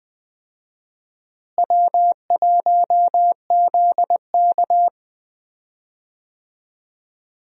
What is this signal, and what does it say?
Morse code, heard as a keyed tone, about 20 wpm, 700 Hz. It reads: W1ZK
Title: USA Call Signs - 4 Characters - Repeated 3x - 20wpm